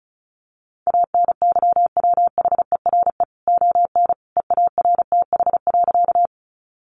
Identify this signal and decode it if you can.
Morse code, heard as a keyed tone, about 34 wpm, 700 Hz. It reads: ANYWHERE ON EARTH.
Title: Sentences from Top 1400 Words 35wpm